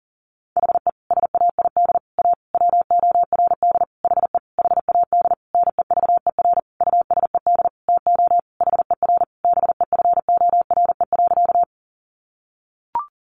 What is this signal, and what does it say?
Morse code, heard as a keyed tone, about 40 wpm, 700 Hz. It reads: HE SAID A WORD HE HAD NEVER USED TO HER BEFORE.
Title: Sentences from Top 600 Words 40wpm